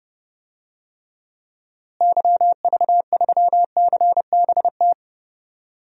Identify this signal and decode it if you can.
Morse code, heard as a keyed tone, about 32 wpm, 700 Hz. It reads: YV3CBT
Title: DXCC-100 Call Signs - 6 Characters - Repeated 3x - 30wpm